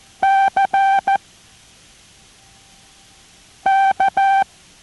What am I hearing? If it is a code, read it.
Morse code, heard as a keyed tone, about 14 wpm, 780 Hz. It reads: C K